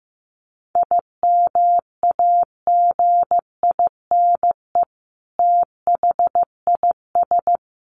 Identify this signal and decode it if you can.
Morse code, heard as a keyed tone, about 15 wpm, 700 Hz. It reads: IMAGINE THIS
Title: Sentences from Top 1400 Words 15wpm